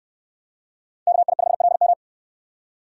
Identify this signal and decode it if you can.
Morse code, heard as a keyed tone, about 68 wpm, 700 Hz. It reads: KI5LM